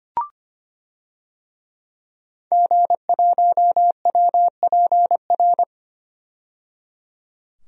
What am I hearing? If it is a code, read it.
Morse code, heard as a keyed tone, about 25 wpm, 700 Hz. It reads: G1WPR
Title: DXCC-100 Call Signs - 5 Characters - 25wpm